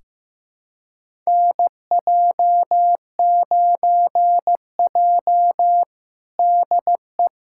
Morse code, heard as a keyed tone, about 15 wpm, 700 Hz. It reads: NJ9J DE
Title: USA Callsign DE Callsign - 4 Characters - 15wpm